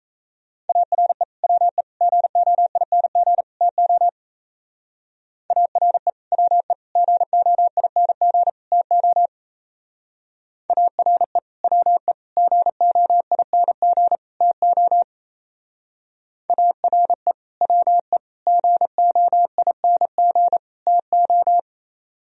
Morse code, heard as a keyed tone, about 36 wpm, 700 Hz. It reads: ARE WE GOING TO ARE WE GOING TO ARE WE GOING TO ARE WE GOING TO